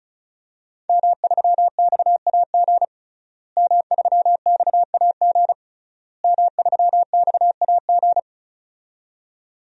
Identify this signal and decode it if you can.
Morse code, heard as a keyed tone, about 34 wpm, 700 Hz. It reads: M3XAG M3XAG M3XAG
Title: DXCC-100 Call Signs - 5 Characters - Repeated 3x - 35wpm